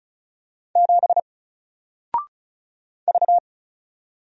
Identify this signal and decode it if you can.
Morse code, heard as a keyed tone, about 36 wpm, 700 Hz. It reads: 7 V
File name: Single-Letter-Number-35wpm